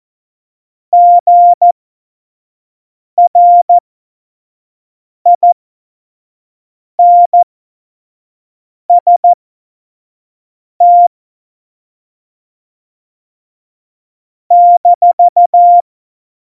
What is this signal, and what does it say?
Morse code, heard as a keyed tone, about 14 wpm, spaced as for 5 wpm, 700 Hz. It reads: GRINST -
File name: Morsekurs / Witze / Witz 26. Januar 2021, 25 F70